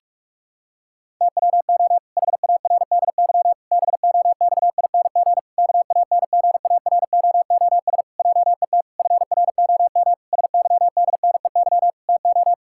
Morse code, heard as a keyed tone, about 46 wpm, 700 Hz. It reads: TWO HARDY BOXING KANGAROOS JET FROM SYDNEY TO
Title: Pangram Challenge v2 - 45wpm